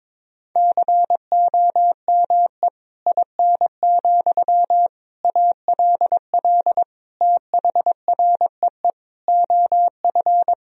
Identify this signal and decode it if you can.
Morse code, heard as a keyed tone, about 22 wpm, 700 Hz. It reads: COME IN, ALL THREE OF
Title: Sentences from Top 1000 Words - Review Part 1 - 22wpm